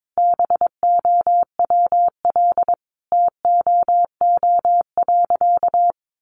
Morse code, heard as a keyed tone, about 22 wpm, 700 Hz. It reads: BOWL TOO.
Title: Sentences from Top 1200 Words 22wpm